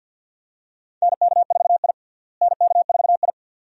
Morse code, heard as a keyed tone, about 52 wpm, 700 Hz. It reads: NK4I NK4I
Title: USA Call Signs - 4 Characters - Repeated 2x - 50wpm